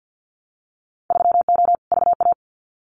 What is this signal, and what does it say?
Morse code, heard as a keyed tone, about 50 wpm, 700 Hz. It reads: 3O VA
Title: Winter Field Day Exchanges 50wpm